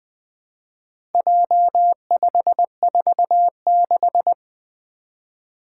Morse code, heard as a keyed tone, about 20 wpm, 700 Hz. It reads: J546
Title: Four Challenging Letter-Number Combinations 20wpm